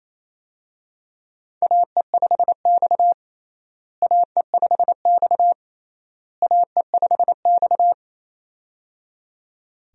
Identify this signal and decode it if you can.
Morse code, heard as a keyed tone, about 28 wpm, 700 Hz. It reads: AE5X AE5X AE5X
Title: USA Call Signs - 4 Characters - Repeated 3x - 28wpm